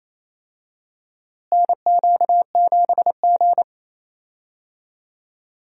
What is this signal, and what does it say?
Morse code, heard as a keyed tone, about 27 wpm, 700 Hz. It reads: NQ7G
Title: USA Call Signs - 4 Characters - Repeated 3x - 28wpm